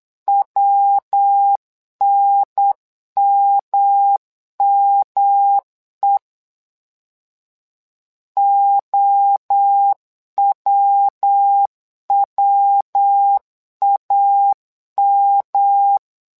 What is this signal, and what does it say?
Morse code, heard as a keyed tone, about 8 wpm, 800 Hz. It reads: WNMME OWWAM
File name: LEZIONI / 3 MWRO / 3 MWRO 8wpm 2606